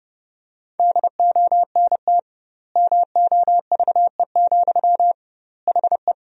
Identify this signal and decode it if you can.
Morse code, heard as a keyed tone, about 30 wpm, 700 Hz. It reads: DONT MOVE, HE
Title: 2x Word-Spacing - Sentences from Top 300 Words - 30wpm